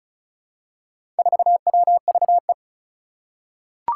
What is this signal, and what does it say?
Morse code, heard as a keyed tone, about 36 wpm, 700 Hz. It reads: 4WVE
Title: Four Challenging Letter-Number Combinations 35wpm